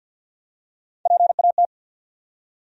Morse code, heard as a keyed tone, about 50 wpm, 700 Hz. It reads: PAT
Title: USA Common Names - Max 5 Letters - 50wpm